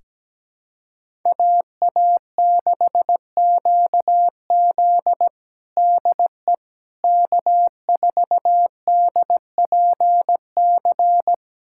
Morse code, heard as a keyed tone, about 17 wpm, 700 Hz. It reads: AA6QZ DE K4DPC
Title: USA Callsign DE Callsign - 5 Characters - 17wpm